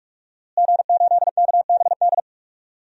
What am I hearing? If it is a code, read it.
Morse code, heard as a keyed tone, about 42 wpm, 700 Hz. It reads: G8KBD
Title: DXCC-100 Call Signs - 5 Characters - Repeated 2x - 45wpm